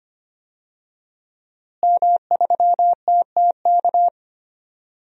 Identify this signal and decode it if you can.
Morse code, heard as a keyed tone, about 25 wpm, 700 Hz. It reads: M3TTK
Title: DXCC-100 Call Signs - 5 Characters - Repeated 2x - 25wpm